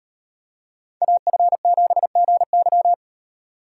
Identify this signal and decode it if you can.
Morse code, heard as a keyed tone, about 38 wpm, 700 Hz. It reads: AF7GY